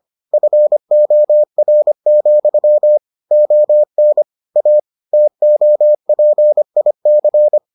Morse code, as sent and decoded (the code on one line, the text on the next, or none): ..-. --- .-. --..-- / --- -. / .- / - --- .--. .. -.-.
FOR, ON A TOPIC